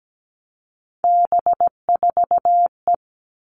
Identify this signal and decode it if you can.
Morse code, heard as a keyed tone, about 17 wpm, 700 Hz. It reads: B4E
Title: All Cut Numbers - 3-Digit 17wpm